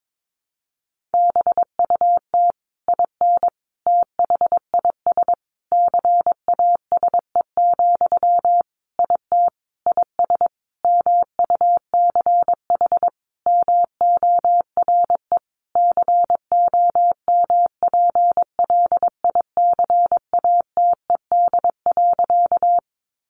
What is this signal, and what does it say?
Morse code, heard as a keyed tone, about 22 wpm, 700 Hz. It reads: BUT IN THIS CASE, IT IS MUCH MORE COMPLICATED.
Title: Sentences from Top 1200 Words 22wpm